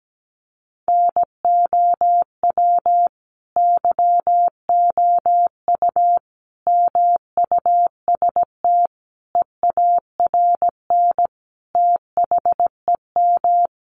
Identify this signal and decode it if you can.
Morse code, heard as a keyed tone, about 17 wpm, 700 Hz. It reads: NOW YOU MUST EARN THEM
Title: Sentences from Top 1500 Words 17wpm